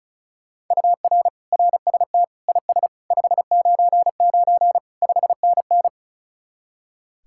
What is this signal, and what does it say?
Morse code, heard as a keyed tone, about 45 wpm, 700 Hz. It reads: UR RST IS 599 5NN